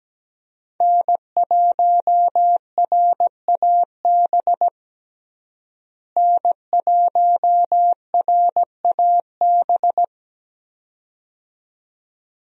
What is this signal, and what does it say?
Morse code, heard as a keyed tone, about 17 wpm, 700 Hz. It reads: N1RAB N1RAB
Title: USA Call Signs - 5 Characters - Repeated 2x - 17wpm